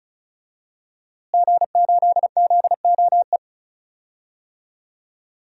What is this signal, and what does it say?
Morse code, heard as a keyed tone, about 34 wpm, 700 Hz. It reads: G8ZOE
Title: DXCC-100 Call Signs - 5 Characters - Repeated 2x - 35wpm